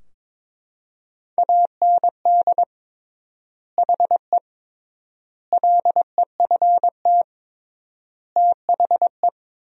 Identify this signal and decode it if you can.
Morse code, heard as a keyed tone, about 22 wpm, 700 Hz. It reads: AND HE LEFT THE
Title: 3x Word-Spacing - Sentences from Top 300 Words - 22wpm